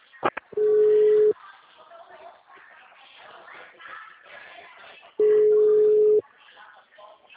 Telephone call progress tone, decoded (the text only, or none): ringback tone